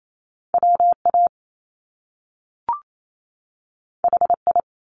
Morse code, heard as a keyed tone, about 28 wpm, 700 Hz. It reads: WA HI
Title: US States 28wpm